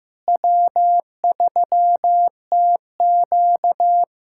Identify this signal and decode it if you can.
Morse code, heard as a keyed tone, about 15 wpm, 700 Hz. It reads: W3TQ
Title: USA Callsign DE Callsign - 4 Characters - 15wpm